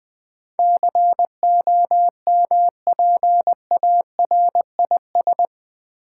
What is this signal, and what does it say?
Morse code, heard as a keyed tone, about 20 wpm, 700 Hz. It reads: COMPARIS
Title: Build A Word - 4 Letters At A Time - 10-Letter Words - 20wpm